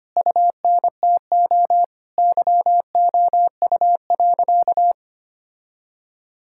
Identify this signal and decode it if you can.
Morse code, heard as a keyed tone, about 25 wpm, 700 Hz. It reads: UNTO YOU.
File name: practice / 25wpm / Sentences with 9-Word Phrases - 25wpm